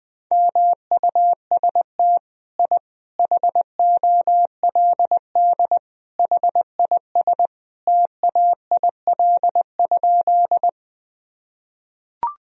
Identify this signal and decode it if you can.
Morse code, heard as a keyed tone, about 20 wpm, 700 Hz. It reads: MUST I HOLD HIS TAIL?
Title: Sentences from Top 1000 Words - Review Part 3 - 20wpm